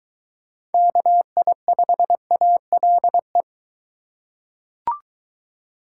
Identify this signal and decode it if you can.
Morse code, heard as a keyed tone, about 21 wpm, 700 Hz. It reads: KI5ALE